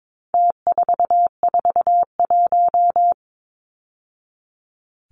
Morse code, heard as a keyed tone, about 22 wpm, 700 Hz. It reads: T441
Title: Numbers - Contest Style - 4-Digit 22wpm